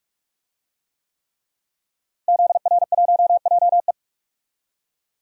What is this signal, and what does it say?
Morse code, heard as a keyed tone, about 46 wpm, 700 Hz. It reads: ZR1JE
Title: DXCC-100 Call Signs - 5 Characters - Repeated 3x - 45wpm